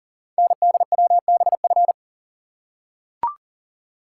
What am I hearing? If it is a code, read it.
Morse code, heard as a keyed tone, about 39 wpm, 700 Hz. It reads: NDWBF
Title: Code Groups - 5 Letters - 40wpm